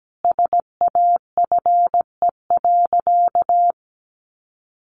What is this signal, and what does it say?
Morse code, heard as a keyed tone, about 17 wpm, 700 Hz. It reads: SAFE.
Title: Additional Easy Phrases 17wpm